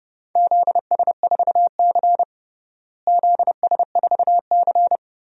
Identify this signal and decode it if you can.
Morse code, heard as a keyed tone, about 30 wpm, 700 Hz. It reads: ZS4C ZS4C
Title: DXCC-100 Call Signs - 4 Characters - Repeated 3x - 30wpm